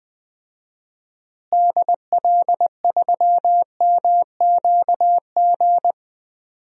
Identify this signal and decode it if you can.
Morse code, heard as a keyed tone, about 20 wpm, 700 Hz. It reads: DL3MQG
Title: DXCC-100 Call Signs - 6 Characters - Repeated 3x - 20wpm